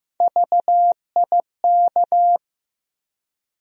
Morse code, heard as a keyed tone, about 15 wpm, 700 Hz. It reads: VIK
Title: USA Call Sign Postfixes - 3 Letters 15wpm